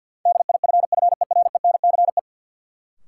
Morse code, heard as a keyed tone, about 53 wpm, 700 Hz. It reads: DIFFERENCE
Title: Sentences from Top 800 Words - New Word Drill - 50wpm